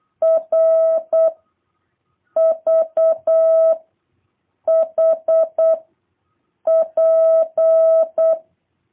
Morse code, heard as a keyed tone, about 8 wpm, 660 Hz. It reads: R V H P